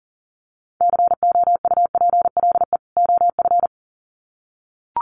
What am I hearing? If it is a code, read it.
Morse code, heard as a keyed tone, about 40 wpm, 700 Hz. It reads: COUPLE OF
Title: Sets of 2 Words 40wpm